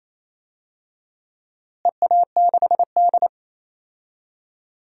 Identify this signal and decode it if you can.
Morse code, heard as a keyed tone, about 28 wpm, 700 Hz. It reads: EA6D
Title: DXCC-100 Call Signs - 4 Characters - Repeated 3x - 28wpm